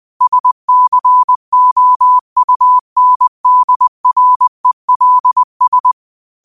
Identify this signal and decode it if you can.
Morse code, heard as a keyed tone, about 20 wpm, 1000 Hz. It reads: SCOUNDRELS
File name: qotdfiles / 20wpm